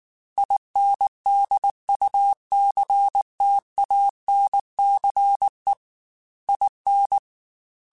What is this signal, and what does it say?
Morse code, heard as a keyed tone, about 19 wpm, 800 Hz. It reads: INDUCTANCE IN